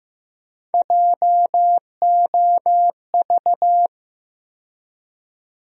Morse code, heard as a keyed tone, about 15 wpm, 700 Hz. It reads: JOV
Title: USA Call Sign Postfixes - 3 Letters 15wpm